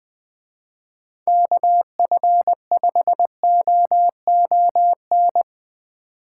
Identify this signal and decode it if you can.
Morse code, heard as a keyed tone, about 20 wpm, 700 Hz. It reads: KF5OON
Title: Speed Racing - USA 2x3 Call Signs 17wpm